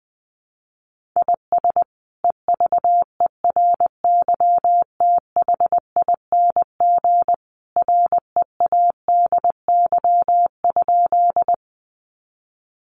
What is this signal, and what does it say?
Morse code, heard as a keyed tone, about 20 wpm, 700 Hz. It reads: IS EVERYTHING READY?
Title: Sentences from Top 700 Words 20wpm